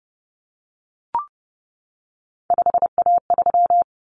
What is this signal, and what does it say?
Morse code, heard as a keyed tone, about 30 wpm, 700 Hz. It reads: 5A3
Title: Common Cut Numbers - 3-Digit 30wpm